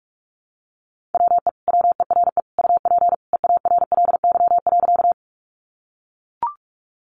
Morse code, heard as a keyed tone, about 44 wpm, 700 Hz. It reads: WE WERE UP EARLY.
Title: Sentences from Top 400 Words 45wpm